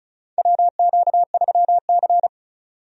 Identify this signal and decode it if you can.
Morse code, heard as a keyed tone, about 36 wpm, 700 Hz. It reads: WQ3C
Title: USA Call Signs - 4 Characters - 35wpm